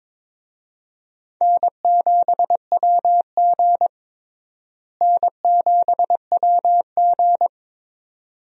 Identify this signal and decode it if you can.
Morse code, heard as a keyed tone, about 22 wpm, 700 Hz. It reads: N7WG N7WG
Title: USA Call Signs - 4 Characters - Repeated 2x - 22wpm